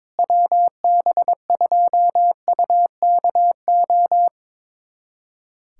Morse code, heard as a keyed tone, about 22 wpm, 700 Hz. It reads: WB2UKO
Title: USA Callsign DE Callsign - 6 Characters - 22wpm